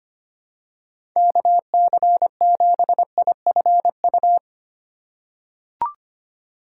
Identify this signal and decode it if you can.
Morse code, heard as a keyed tone, about 25 wpm, 700 Hz. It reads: KC7IFU
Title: USA Call Signs - 6 Characters - 4x Character Spacing - 25wpm